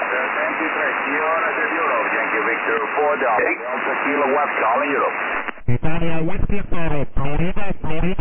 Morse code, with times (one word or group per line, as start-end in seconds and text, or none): none